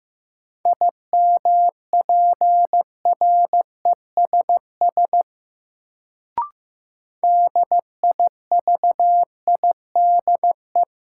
Morse code, heard as a keyed tone, about 15 wpm, 700 Hz. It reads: IMPRESS DIVIDE
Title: Sentences from Top 1600 Words - New Word Drill Alt - 15wpm